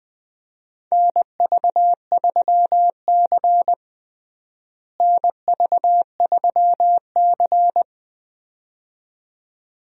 Morse code, heard as a keyed tone, about 20 wpm, 700 Hz. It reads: NV3C NV3C
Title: USA Call Signs - 4 Characters - Repeated 2x - 20wpm